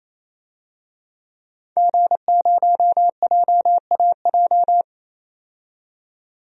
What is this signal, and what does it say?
Morse code, heard as a keyed tone, about 28 wpm, 700 Hz. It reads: G0JAJ
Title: DXCC-100 Call Signs - 5 Characters - 28wpm